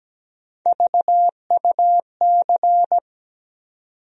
Morse code, heard as a keyed tone, about 17 wpm, 700 Hz. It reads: VUC
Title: USA Call Sign Postfixes - 3 Letters 17wpm